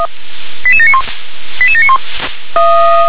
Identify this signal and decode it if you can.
Five-tone selective call (ZVEI), heard as quick runs of five tones, tones of about 70 ms. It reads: 88791 88791